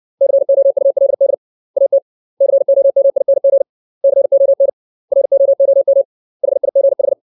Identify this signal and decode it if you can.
Morse code, heard as a keyed tone, about 60 wpm, 550 Hz. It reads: COULD AT COMING CON ROOM HIGH
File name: 632 words / 632 Words 60WPM